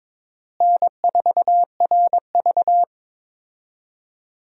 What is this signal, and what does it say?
Morse code, heard as a keyed tone, about 22 wpm, 700 Hz. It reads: N4RV
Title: USA Call Signs - 4 Characters - Repeated 3x - 22wpm